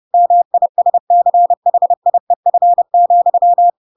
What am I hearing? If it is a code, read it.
Morse code, heard as a keyed tone, about 30 wpm, 700 Hz. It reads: MISCHIEF,
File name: Pinocchio / Pinocchio ch8 30 30 ik1hge